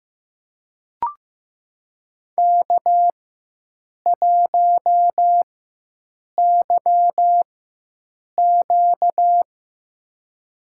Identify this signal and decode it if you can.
Morse code, heard as a keyed tone, about 15 wpm, spaced as for 7 wpm, 700 Hz. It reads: K1YQ
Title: USA Call Signs - 4 Characters - 4x Character Spacing - 15wpm